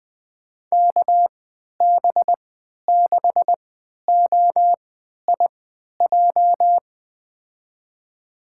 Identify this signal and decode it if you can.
Morse code, heard as a keyed tone, about 20 wpm, spaced as for 11 wpm, 700 Hz. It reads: KB6OIJ
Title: USA Call Signs - 6 Characters - 3x Character Spacing - 20wpm